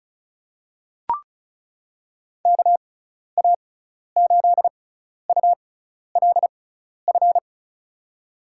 Morse code, heard as a keyed tone, about 34 wpm, spaced as for 12 wpm, 700 Hz. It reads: KA8ULF
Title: USA Call Signs - 6x Character Spacing - 35wpm